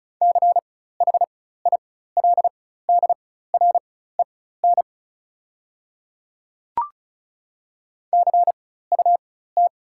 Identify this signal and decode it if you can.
Morse code, heard as a keyed tone, about 36 wpm, spaced as for 16 wpm, 700 Hz. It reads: CHILDREN CUT
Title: 4x Character Spacing - Top 300 Words - 35wpm